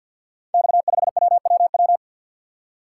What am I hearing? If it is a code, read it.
Morse code, heard as a keyed tone, about 45 wpm, 700 Hz. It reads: X5WWW
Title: Code Groups - 5 Letter:Numbers - 50wpm